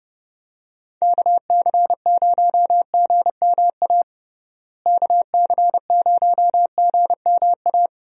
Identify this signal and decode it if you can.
Morse code, heard as a keyed tone, about 30 wpm, 700 Hz. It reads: KC0GMA KC0GMA
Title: USA Call Signs - 6 Characters - Repeated 2x - 30wpm